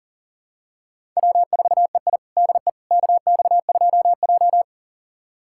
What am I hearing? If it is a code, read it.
Morse code, heard as a keyed tone, about 40 wpm, 700 Hz. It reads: W4EI DE KX2J
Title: USA Callsign DE Callsign - 4 Characters - 40wpm